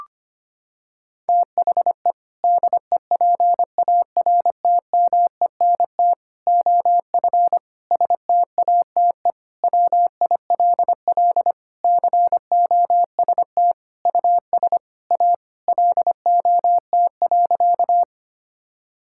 Morse code, heard as a keyed tone, about 25 wpm, 700 Hz. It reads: THE DEPARTMENT OF STATE WILL COST US A LOT.
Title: Sentences from Top 1000 Words - Review Part 5 - 25wpm